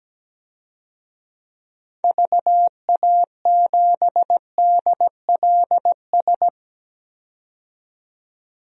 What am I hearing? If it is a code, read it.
Morse code, heard as a keyed tone, about 17 wpm, 700 Hz. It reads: VA7DLS
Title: DXCC-100 Call Signs - 6 Characters - 17wpm